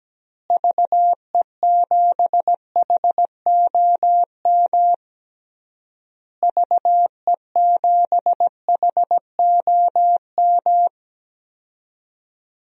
Morse code, heard as a keyed tone, about 17 wpm, 700 Hz. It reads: VE7HOM VE7HOM
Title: DXCC-100 Call Signs - 6 Characters - Repeated 3x - 17wpm